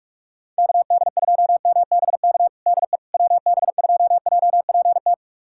Morse code, heard as a keyed tone, about 44 wpm, 700 Hz. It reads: KD2MBK DE WB2JPT
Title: USA Callsign DE Callsign 45wpm